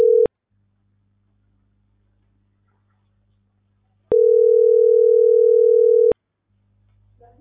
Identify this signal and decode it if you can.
Ringback tone (North America), 440 + 480 Hz.